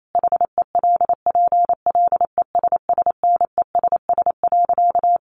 Morse code, heard as a keyed tone, about 28 wpm, 700 Hz. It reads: HELPLESSNESS.
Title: A Sound Of Thunder 28wpm